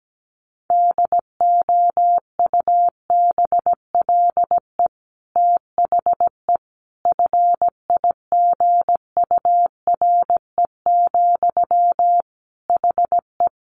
Morse code, heard as a keyed tone, about 17 wpm, 700 Hz. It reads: DOUBLE THE FIGURE, HE
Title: Sentences from Top 1400 Words 17wpm